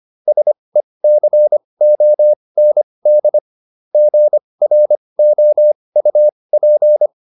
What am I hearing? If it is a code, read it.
Morse code, heard as a keyed tone, about 25 wpm, spaced as for 20 wpm, 600 Hz. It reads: SECOND GROUP